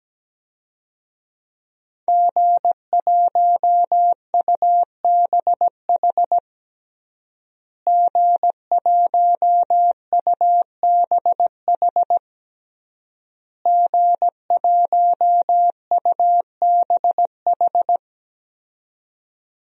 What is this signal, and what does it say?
Morse code, heard as a keyed tone, about 17 wpm, 700 Hz. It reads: G1UBH G1UBH G1UBH
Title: DXCC-100 Call Signs - 5 Characters - Repeated 3x - 17wpm